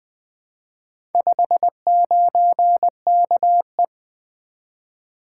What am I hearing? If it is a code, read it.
Morse code, heard as a keyed tone, about 20 wpm, 700 Hz. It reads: 59KE
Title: Four Letter-Number Combinations - 20wpm